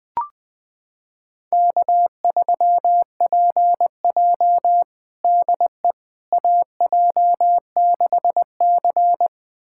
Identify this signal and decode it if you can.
Morse code, heard as a keyed tone, about 21 wpm, 700 Hz. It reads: K3PJ DE AJ6C
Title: USA Callsign DE Callsign - 4 Characters - 20wpm